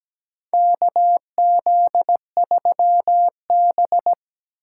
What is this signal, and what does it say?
Morse code, heard as a keyed tone, about 17 wpm, 700 Hz. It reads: KZ3B
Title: USA Call Signs - 4 Characters - 2x Character Spacing - 17wpm